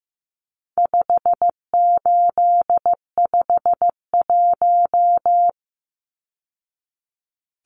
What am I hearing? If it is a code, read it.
Morse code, heard as a keyed tone, about 15 wpm, 700 Hz. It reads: 5851
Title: Numbers - 4-Digit 15wpm